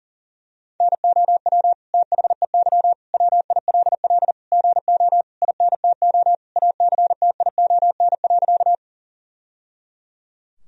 Morse code, heard as a keyed tone, about 40 wpm, 700 Hz. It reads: NOW THEY WILL GO INTO ACTION.
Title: Sentences from Top 1000 Words - Review Part 4 - 40wpm